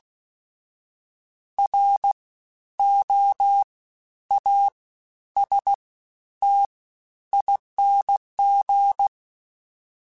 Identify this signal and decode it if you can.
Morse code, heard as a keyed tone, about 16 wpm, 800 Hz. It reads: R O A S T ING
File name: Special INGs-mp3s / Special INGs-mp3s / Fast ing @15